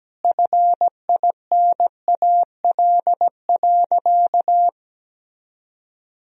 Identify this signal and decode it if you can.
Morse code, heard as a keyed tone, about 17 wpm, 700 Hz. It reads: FINAL.
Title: Sentences from Top 1000 Words - Review Part 2 - 17wpm